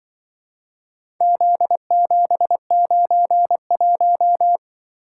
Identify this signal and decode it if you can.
Morse code, heard as a keyed tone, about 24 wpm, 700 Hz. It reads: Z791
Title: Speed-Racing - Four Letter-Number Combinations - 20wpm